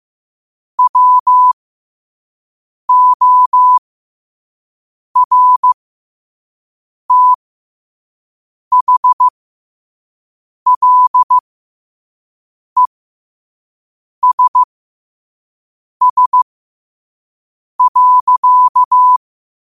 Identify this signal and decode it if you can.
Morse code, heard as a keyed tone, about 15 wpm, spaced as for 5 wpm, 1000 Hz. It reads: WORTHLESS.
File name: qotdfiles / 5wpm